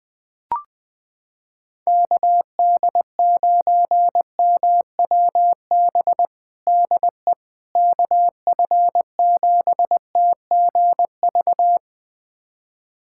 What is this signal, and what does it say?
Morse code, heard as a keyed tone, about 20 wpm, 700 Hz. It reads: KD9MWB DE KF7TGV
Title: USA Callsign DE Callsign - 6 Characters - 20wpm